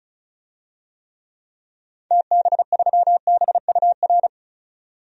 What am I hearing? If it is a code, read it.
Morse code, heard as a keyed tone, about 36 wpm, 700 Hz. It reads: TB3BUR
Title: DXCC-100 Call Signs - 6 Characters - 35wpm